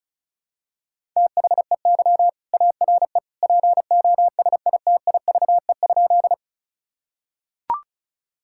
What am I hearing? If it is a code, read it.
Morse code, heard as a keyed tone, about 36 wpm, 700 Hz. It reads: THEY ARE POSITIVE?
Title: Sentences from Top 1000 Words - Review Part 5 - 35wpm